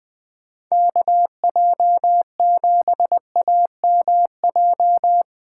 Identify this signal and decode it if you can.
Morse code, heard as a keyed tone, about 20 wpm, 700 Hz. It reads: KJ7AMJ
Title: USA Call Signs - 6 Characters - 4x Character Spacing - 20wpm